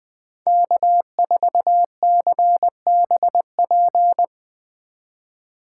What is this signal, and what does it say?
Morse code, heard as a keyed tone, about 20 wpm, 700 Hz. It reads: K4CBP
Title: Speed Racing - USA 1x3 Call Signs 17wpm